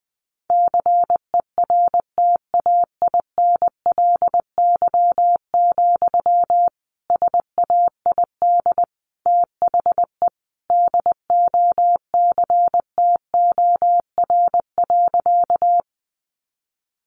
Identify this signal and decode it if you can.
Morse code, heard as a keyed tone, about 20 wpm, 700 Hz. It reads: CERTAINLY, SAID THE DOCTOR.
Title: Sentences from Top 1500 Words 20wpm